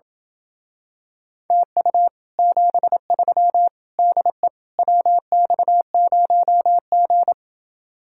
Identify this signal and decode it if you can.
Morse code, heard as a keyed tone, about 28 wpm, 700 Hz. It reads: TU 73 DE WX0G